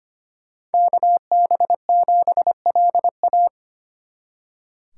Morse code, heard as a keyed tone, about 31 wpm, 700 Hz. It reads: KB7LA
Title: Speed Racing - USA 2x2 Call Signs 25wpm